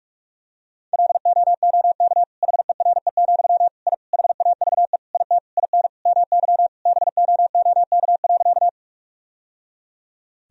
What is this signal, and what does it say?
Morse code, heard as a keyed tone, about 46 wpm, 700 Hz. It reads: LOOK HERE, I HAVE IT IN MY BOOK.
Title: Sentences from Top 1000 Words - Review Part 1 - 45wpm